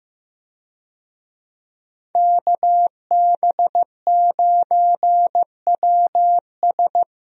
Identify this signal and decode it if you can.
Morse code, heard as a keyed tone, about 15 wpm, 700 Hz. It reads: KB9WS
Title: USA Call Signs - 5 Characters - Repeated 2x - 15wpm